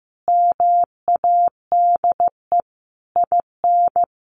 Morse code, heard as a keyed tone, about 15 wpm, 700 Hz. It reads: MADE IN
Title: Sentences from Top 600 Words 15wpm